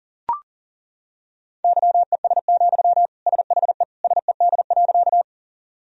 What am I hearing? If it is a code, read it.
Morse code, heard as a keyed tone, about 37 wpm, 700 Hz. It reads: YES, SHE SED.
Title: Sentences from Top 500 Words and Abbreviations 40wpm